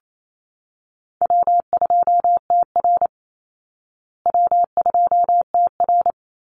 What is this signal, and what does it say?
Morse code, heard as a keyed tone, about 28 wpm, 700 Hz. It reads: W2TR W2TR
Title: Easy Call Signs Repeated 2x - 28wpm